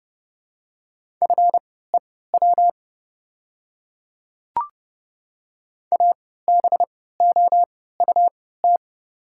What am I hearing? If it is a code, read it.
Morse code, heard as a keyed tone, about 32 wpm, spaced as for 17 wpm, 700 Hz. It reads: FEW ABOUT
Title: 3x Character Spacing - Top 300 Words - 30wpm